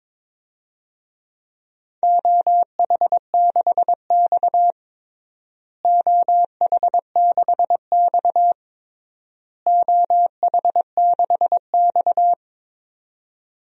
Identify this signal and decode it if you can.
Morse code, heard as a keyed tone, about 22 wpm, 700 Hz. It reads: OH6X OH6X OH6X
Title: DXCC-100 Call Signs - 4 Characters - Repeated 3x - 22wpm